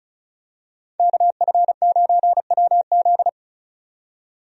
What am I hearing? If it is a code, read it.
Morse code, heard as a keyed tone, about 36 wpm, 700 Hz. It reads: KF9WZ
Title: USA Call Signs - 5 Characters - 35wpm